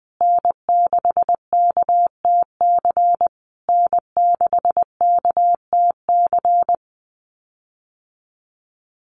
Morse code, heard as a keyed tone, about 20 wpm, 700 Hz. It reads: N6KTC N6KTC
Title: USA Call Signs Repeated 3x 20wpm